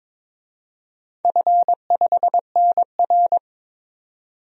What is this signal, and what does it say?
Morse code, heard as a keyed tone, about 22 wpm, 700 Hz. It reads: F5NR
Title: DXCC-100 Call Signs - 4 Characters - 22wpm